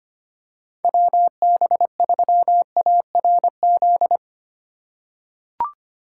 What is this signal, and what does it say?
Morse code, heard as a keyed tone, about 25 wpm, 700 Hz. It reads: WB3ARZ
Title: USA Call Signs - 6 Characters - 4x Character Spacing - 25wpm